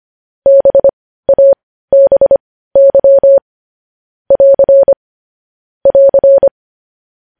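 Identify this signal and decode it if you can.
Morse code, heard as a keyed tone, about 25 wpm, spaced as for 15 wpm, 550 Hz. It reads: BABY <AR> <AR>
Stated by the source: News headlines sent in Morse code at 15 wpm.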